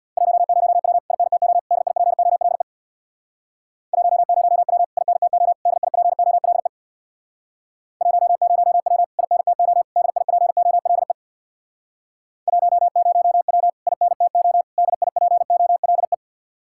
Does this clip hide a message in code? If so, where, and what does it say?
Yes: Morse code from 0.2 to 16.1 s, heard as a keyed tone, about 64 wpm, 700 Hz. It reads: 10W INTO DIPOLE 10W INTO DIPOLE 10W INTO DIPOLE 10W INTO DIPOLE